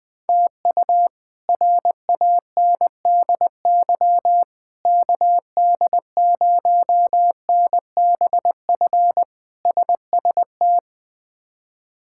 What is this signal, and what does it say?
Morse code, heard as a keyed tone, about 20 wpm, 700 Hz. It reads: TU RANDY KD0NBF SST
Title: K1USN SST QSO Practice - 20wpm